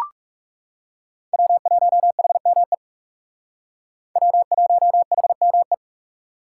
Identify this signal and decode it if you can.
Morse code, heard as a keyed tone, about 37 wpm, 700 Hz. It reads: W1HME W1HME
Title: Speed Racing - USA 1x3 Call Signs 30wpm